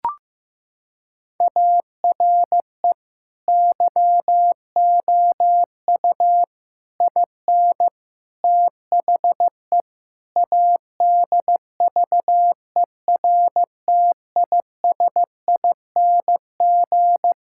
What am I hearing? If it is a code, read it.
Morse code, heard as a keyed tone, about 15 wpm, 700 Hz. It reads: ARE YOU IN THE ADVERTISING
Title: Sentences from Top 1000 Words - Review Part 6 - 15wpm